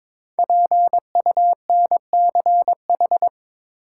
Morse code, heard as a keyed tone, about 22 wpm, 700 Hz. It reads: PUNCH
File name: 5 Letter Words - First 3 Letters Given - 22wpm